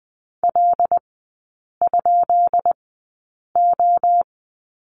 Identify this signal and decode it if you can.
Morse code, heard as a keyed tone, about 20 wpm, spaced as for 8 wpm, 700 Hz. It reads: L?O
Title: 159 Lesson 26e - Letter g - Three Characters 20wpm